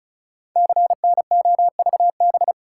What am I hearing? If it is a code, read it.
Morse code, heard as a keyed tone, about 36 wpm, 700 Hz. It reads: CNOVB
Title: Code Groups - 5 Letters - 35wpm